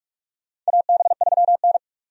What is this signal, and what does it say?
Morse code, heard as a keyed tone, about 44 wpm, 700 Hz. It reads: AB3N
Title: USA Call Signs - 4 Characters - 45wpm